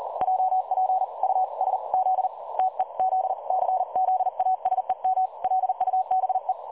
Morse code, heard as a keyed tone, about 37 wpm, 750 Hz. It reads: JP US Z TEZ CZASEM LUBE
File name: dlazaawansowanych / nagraniazpasma / websdr recording 3565.0kHz